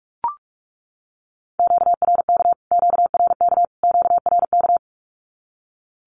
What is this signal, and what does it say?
Morse code, heard as a keyed tone, about 44 wpm, 700 Hz. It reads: QRK QRK QRK
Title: Q-Codes Repeated 3x 45wpm